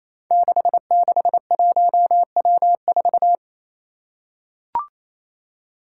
Morse code, heard as a keyed tone, about 28 wpm, 700 Hz. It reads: 661W4
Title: Five Challenging Letter-Number Combinations 28wpm